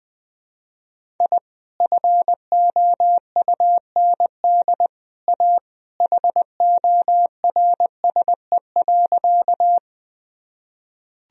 Morse code, heard as a keyed tone, about 21 wpm, 700 Hz. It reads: I FOUND A HORSE.
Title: Sentences from Top 1000 Words - Review Part 1 - 20wpm